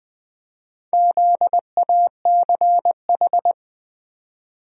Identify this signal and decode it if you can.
Morse code, heard as a keyed tone, about 20 wpm, 700 Hz. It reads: ZACH
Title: USA Common Names - Max 4 Letters - 20wpm